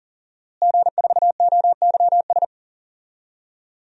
Morse code, heard as a keyed tone, about 40 wpm, 700 Hz. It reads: G4OYS
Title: DXCC-100 Call Signs - 5 Characters - Repeated 2x - 40wpm